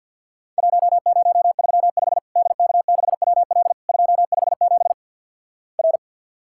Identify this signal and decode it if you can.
Morse code, heard as a keyed tone, about 50 wpm, 700 Hz. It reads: 1035 DK6WL 257 R
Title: Worked All Europe DX Contest - QTC Exchange - 50wpm